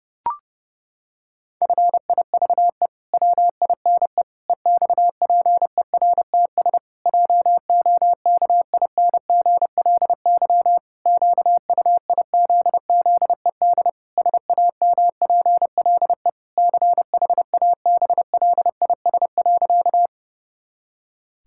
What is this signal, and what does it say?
Morse code, heard as a keyed tone, about 32 wpm, 700 Hz. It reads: FIVE WINE EXPERTS JOKINGLY QUIZZED SAMPLE CHABLIS.
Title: Pangram Challenge v2 - 30wpm